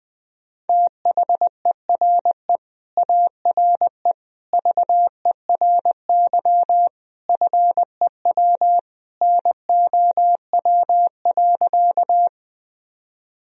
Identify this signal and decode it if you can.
Morse code, heard as a keyed tone, about 20 wpm, 700 Hz. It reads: THERE ARE VERY FEW NOW.
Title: Sentences from Top 1000 Words - Review Part 1 - 20wpm